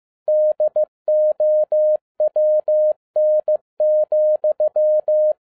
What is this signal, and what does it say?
Morse code, heard as a keyed tone, about 15 wpm, 610 Hz. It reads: DOWN,
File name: 2011-10-mp3 / n07-A DOT 15wpm.mp3